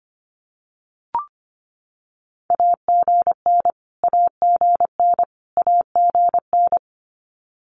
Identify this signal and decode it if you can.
Morse code, heard as a keyed tone, about 25 wpm, 700 Hz. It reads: AGN AGN AGN
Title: Abbreviations Repeated 3x - 25wpm